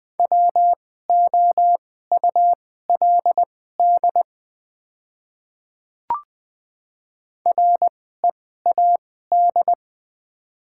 Morse code, heard as a keyed tone, about 20 wpm, spaced as for 14 wpm, 690 Hz. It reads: WOULD READ
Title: 2x Character Spacing - Top 300 Words - 20wpm